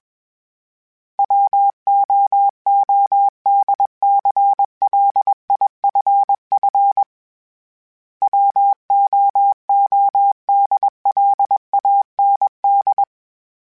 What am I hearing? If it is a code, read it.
Morse code, heard as a keyed tone, about 20 wpm, 800 Hz. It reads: WOODCLIFF WOODLAND
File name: fromSMs / FromSMs mp3s / Cities@20